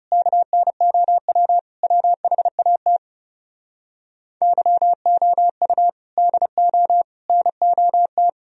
Morse code, heard as a keyed tone, about 40 wpm, 700 Hz. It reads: KNOW WHAT YOU DO NOT